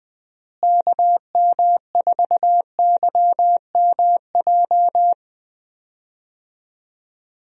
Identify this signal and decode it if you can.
Morse code, heard as a keyed tone, about 20 wpm, 700 Hz. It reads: KM4YMJ
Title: USA Call Signs - 6 Characters - Repeated 2x - 20wpm